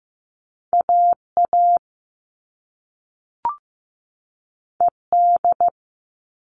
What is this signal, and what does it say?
Morse code, heard as a keyed tone, about 15 wpm, 700 Hz. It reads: AA ED
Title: All Cut Numbers - 2-Digit 15wpm